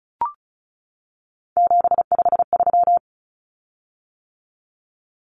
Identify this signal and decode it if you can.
Morse code, heard as a keyed tone, about 34 wpm, 700 Hz. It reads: 753
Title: Common Cut Numbers - 3-Digit 35wpm